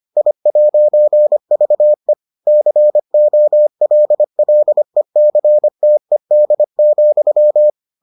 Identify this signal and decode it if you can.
Morse code, heard as a keyed tone, about 25 wpm, 600 Hz. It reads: I'VE COLLECTED,